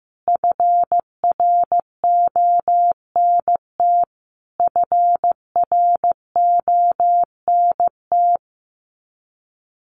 Morse code, heard as a keyed tone, about 15 wpm, 700 Hz. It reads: FRONT FRONT
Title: Top 500 Words Repeated 15wpm